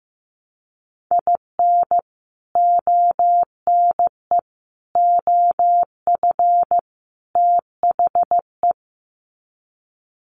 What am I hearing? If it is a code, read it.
Morse code, heard as a keyed tone, about 15 wpm, 700 Hz. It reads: IN ONE OF THE
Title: Sets of 4 Words 15wpm